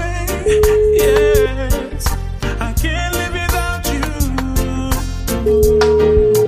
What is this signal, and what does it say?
Ringback tone (Europe, 425 Hz), with noise.